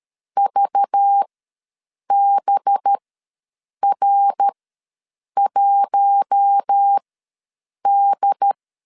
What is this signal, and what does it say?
Morse code, heard as a keyed tone, about 13 wpm, spaced as for 7 wpm, 800 Hz. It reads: VBR1D
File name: LEZIONI / 5 LBVC / 5 LBV ws 12wpm 3015